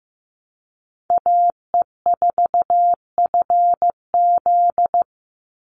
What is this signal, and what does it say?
Morse code, heard as a keyed tone, about 15 wpm, 700 Hz. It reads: AE4FZ
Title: NA SOTA Chaser Callsigns 15wpm